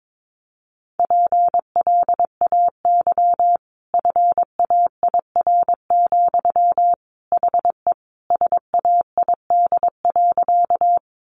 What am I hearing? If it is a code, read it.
Morse code, heard as a keyed tone, about 22 wpm, 700 Hz. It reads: PLAY FAIR, HE SAID.
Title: Sentences from Top 1300 Words 22wpm